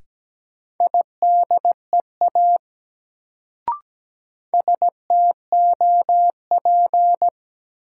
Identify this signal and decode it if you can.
Morse code, heard as a keyed tone, about 17 wpm, 700 Hz. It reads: IDEA STOP
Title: Sentences from Top 300 Words - New Word Drill - 17wpm